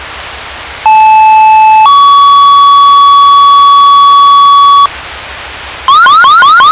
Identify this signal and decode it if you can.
Two-tone page, heard as a one-second tone, then a three-second tone, about 870 then 1130 Hz, with noise. It is heard from 0.8 to 4.9 s.